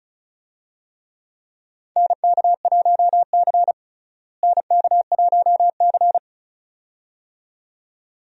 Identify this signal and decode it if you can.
Morse code, heard as a keyed tone, about 34 wpm, 700 Hz. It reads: NK1C NK1C
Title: USA Call Signs - 4 Characters - Repeated 2x - 35wpm